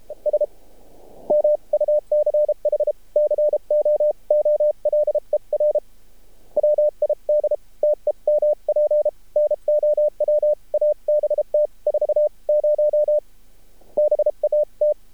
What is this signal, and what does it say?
Morse code, heard as a keyed tone, about 32 wpm, 600 Hz. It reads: S MUCH COOLER WID TEMP NOW ABT 40 BAT
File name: archived / CRSnet